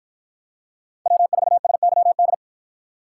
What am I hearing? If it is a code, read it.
Morse code, heard as a keyed tone, about 48 wpm, 700 Hz. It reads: W4SYD